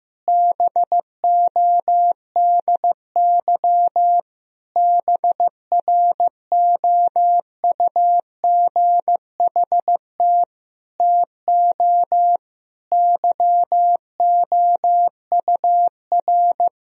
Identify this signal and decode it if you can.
Morse code, heard as a keyed tone, about 16 wpm, 700 Hz. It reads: BODY BROUGHT TO YOUR
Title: Sentences from Top 1000 Words - Review Part 2 - 15wpm